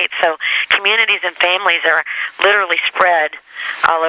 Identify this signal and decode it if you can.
Emergency Alert System (SAME) tones, heard as harsh buzzing bursts, with speech. It is heard from 0.4 to 3.7 s.